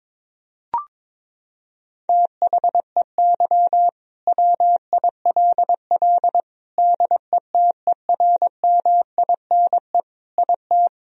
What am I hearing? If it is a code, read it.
Morse code, heard as a keyed tone, about 22 wpm, 700 Hz. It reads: THEY WILL DETERMINE IT
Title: Sentences from Top 1000 Words - Review Part 4 - 22wpm